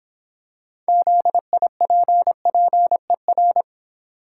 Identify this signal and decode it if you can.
Morse code, heard as a keyed tone, about 26 wpm, 700 Hz. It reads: ZIPPER